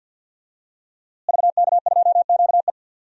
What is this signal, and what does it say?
Morse code, heard as a keyed tone, about 52 wpm, 700 Hz. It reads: VK2QE
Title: DXCC-100 Call Signs - 5 Characters - Repeated 3x - 50wpm